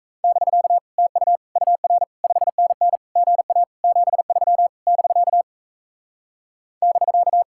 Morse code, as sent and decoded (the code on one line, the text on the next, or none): -...-.- / - ..- / ..- .-. / ..... -. -. / --. .- / --... ...-- / -...-.- / -...-.-
<BK> TU UR 5NN GA 73 <BK> <BK>